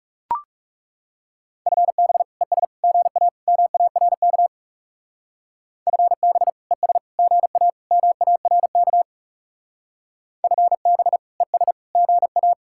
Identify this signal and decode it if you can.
Morse code, heard as a keyed tone, about 37 wpm, 700 Hz. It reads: FB ES GA MARK FB ES GA MARK FB ES GA
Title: Speed-Racing Common QSO Phrases - 30wpm